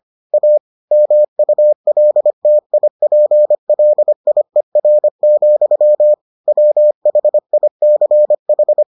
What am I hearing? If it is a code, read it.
Morse code, heard as a keyed tone, about 25 wpm, 600 Hz. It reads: A MULTIPLIER, WHICH